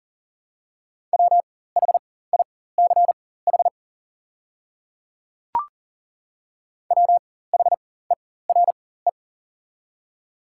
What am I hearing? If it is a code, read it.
Morse code, heard as a keyed tone, about 40 wpm, spaced as for 19 wpm, 690 Hz. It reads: WHICH WHERE
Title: 4x Character Spacing - Top 200 Words - 40wpm